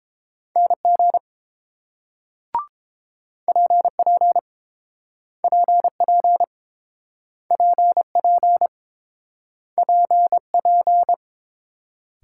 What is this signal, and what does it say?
Morse code, heard as a keyed tone, about 29 wpm, 700 Hz. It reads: NG PP PP PP PP